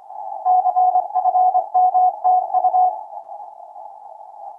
Morse code, heard as a keyed tone, about 24 wpm, 740 Hz. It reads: CFMTU
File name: vk9dlx 22-10-14 1032utc 160m